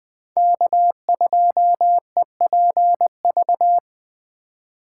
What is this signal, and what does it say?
Morse code, heard as a keyed tone, about 20 wpm, 700 Hz. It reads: K2EPV
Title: USA Call Signs - 5 Characters - 3x Character Spacing - 20wpm